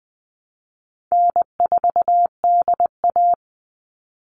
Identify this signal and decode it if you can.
Morse code, heard as a keyed tone, about 20 wpm, 700 Hz. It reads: N4DA
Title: NA SOTA Chaser Callsigns 20wpm